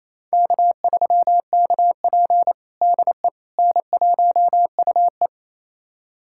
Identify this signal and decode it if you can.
Morse code, heard as a keyed tone, about 28 wpm, 700 Hz. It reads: K3KP DE N1UE
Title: USA Callsign DE Callsign - 4 Characters - 28wpm